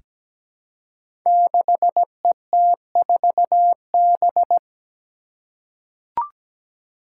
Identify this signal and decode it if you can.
Morse code, heard as a keyed tone, about 17 wpm, 700 Hz. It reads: 6ET4B
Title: Five Challenging Letter-Number Combinations 17wpm